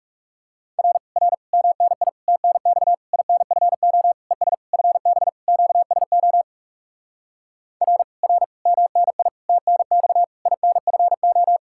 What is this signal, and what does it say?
Morse code, heard as a keyed tone, about 40 wpm, 700 Hz. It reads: R R MNI TNX INFO ES FB QSO R R MNI TNX INFO